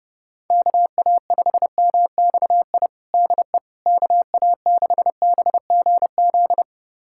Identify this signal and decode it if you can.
Morse code, heard as a keyed tone, about 30 wpm, 700 Hz. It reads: KA5MXI DE KA6BGZ
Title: USA Callsign DE Callsign - 6 Characters - 30wpm